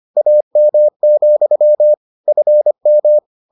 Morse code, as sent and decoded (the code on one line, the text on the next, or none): .- -- --..-- / ..-. --
AM, FM